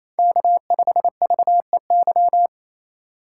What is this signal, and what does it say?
Morse code, heard as a keyed tone, about 28 wpm, 700 Hz. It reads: K5VEY
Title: USA Call Signs - 5 Characters - 2x Character Spacing - 28wpm